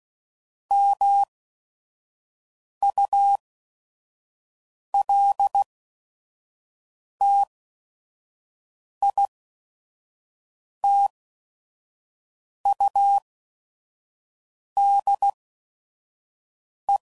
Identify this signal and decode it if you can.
Morse code, heard as a keyed tone, about 16 wpm, spaced as for 5 wpm, 800 Hz. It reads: MULTITUDE
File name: CW Code Groups @5WPM